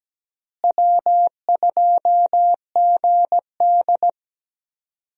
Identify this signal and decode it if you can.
Morse code, heard as a keyed tone, about 17 wpm, 700 Hz. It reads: W2GD
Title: USA Call Signs - 4 Characters - Repeated 2x - 17wpm